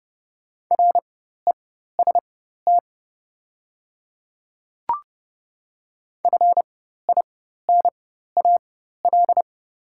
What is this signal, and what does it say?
Morse code, heard as a keyed tone, about 30 wpm, spaced as for 14 wpm, 700 Hz. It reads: REST FINAL
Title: 4x Character Spacing - Top 500 Words - 30wpm